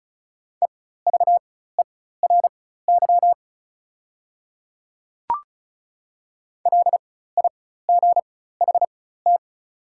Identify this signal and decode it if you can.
Morse code, heard as a keyed tone, about 36 wpm, spaced as for 16 wpm, 700 Hz. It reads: EVERY LIGHT
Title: 4x Character Spacing - Top 200 Words - 35wpm